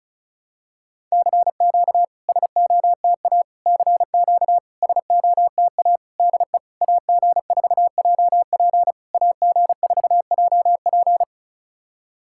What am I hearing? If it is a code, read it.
Morse code, heard as a keyed tone, about 36 wpm, 700 Hz. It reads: CQ SOTA CQ SOTA DE AG4JP AG4JP
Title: SOTA QSO Practice 35wpm